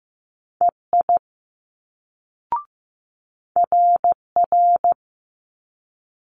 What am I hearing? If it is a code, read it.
Morse code, heard as a keyed tone, about 15 wpm, 700 Hz. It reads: EI RR
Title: Multi-letter Phonemes Quick Fire 15wpm